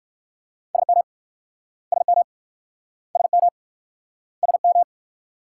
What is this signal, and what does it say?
Morse code, heard as a keyed tone, about 56 wpm, 700 Hz. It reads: SM SM SM SM
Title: Speed-Racing Multi-letter Phonemes - 45wpm